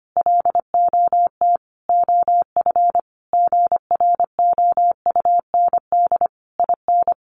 Morse code, heard as a keyed tone, about 25 wpm, 700 Hz. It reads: LOT OF GROUND IN
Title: Sentences from Top 500 Words 25wpm